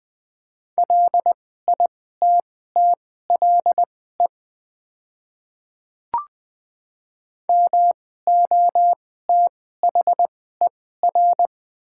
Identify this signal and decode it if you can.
Morse code, heard as a keyed tone, about 20 wpm, spaced as for 15 wpm, 700 Hz. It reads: LITTLE MOTHER
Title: 2x Character Spacing - Top 300 Words - 20wpm